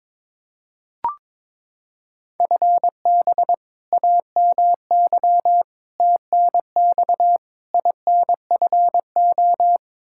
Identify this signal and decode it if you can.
Morse code, heard as a keyed tone, about 22 wpm, 700 Hz. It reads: FB AMY TNX INFO
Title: Practice QSO - Exchange 3 - More Info - 22wpm